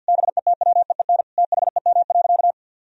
Morse code, heard as a keyed tone, about 52 wpm, 700 Hz. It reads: BETWEEN THEM.
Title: Sentences from Top 400 Words and Abbreviations 50wpm